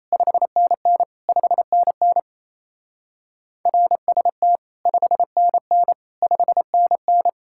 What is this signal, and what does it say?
Morse code, heard as a keyed tone, about 34 wpm, 700 Hz. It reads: 5NN 5NN RST 5NN 5NN
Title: Speed-Racing Common QSO Phrases - 28wpm